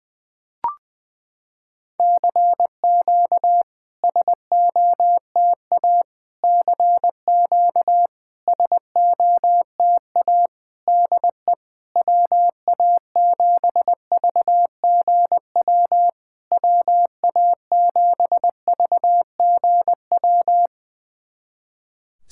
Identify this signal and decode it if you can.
Morse code, heard as a keyed tone, about 20 wpm, 700 Hz. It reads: CQ SOTA CQ SOTA DE WA7VGW WA7VGW
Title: SOTA QSO Practice 20wpm